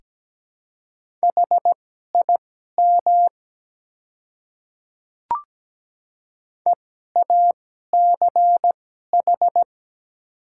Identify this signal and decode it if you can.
Morse code, heard as a keyed tone, about 17 wpm, spaced as for 12 wpm, 690 Hz. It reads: HIM EACH
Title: 2x Character Spacing - Top 300 Words - 17wpm